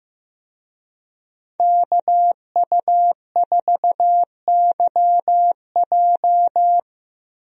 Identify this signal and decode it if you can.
Morse code, heard as a keyed tone, about 15 wpm, 700 Hz. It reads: KU4YJ
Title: USA Call Signs - 5 Characters - Repeated 2x - 15wpm